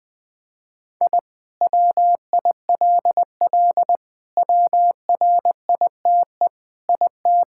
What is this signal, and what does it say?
Morse code, heard as a keyed tone, about 20 wpm, 700 Hz. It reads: I WILL WRITE IT
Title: Sentences with 4-Word Phrases - Spoken Beginning - 20wpm